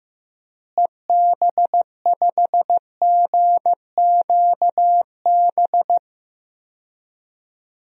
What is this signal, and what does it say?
Morse code, heard as a keyed tone, about 15 wpm, 700 Hz. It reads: EB5GQB
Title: DXCC-100 Call Signs - 6 Characters - Repeated 3x - 15wpm